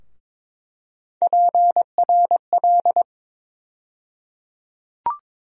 Morse code, heard as a keyed tone, about 22 wpm, 700 Hz. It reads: PRL
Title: USA Call Sign Postfixes - 3 Letters 22wpm